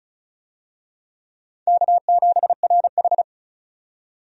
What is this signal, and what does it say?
Morse code, heard as a keyed tone, about 36 wpm, 700 Hz. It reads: K7RH
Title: USA Call Signs - 4 Characters - 35wpm